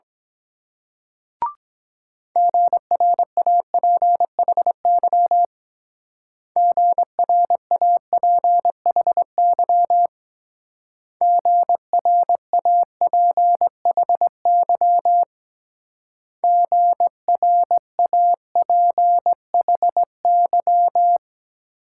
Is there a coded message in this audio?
Yes: Morse code from 2.4 to 21.2 s, heard as a keyed tone, about 22 wpm, 700 Hz. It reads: GRAPHY GRAPHY GRAPHY GRAPHY